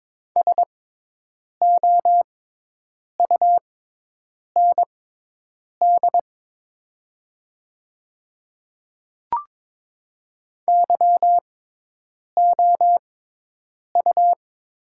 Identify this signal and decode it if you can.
Morse code, heard as a keyed tone, about 22 wpm, spaced as for 8 wpm, 700 Hz. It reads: SOUND YOU
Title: 6x Character Spacing - Top 100 Words - 22wpm